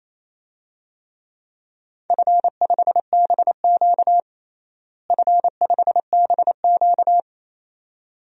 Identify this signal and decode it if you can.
Morse code, heard as a keyed tone, about 28 wpm, 700 Hz. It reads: F5BQ F5BQ
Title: DXCC-100 Call Signs - 4 Characters - Repeated 2x - 28wpm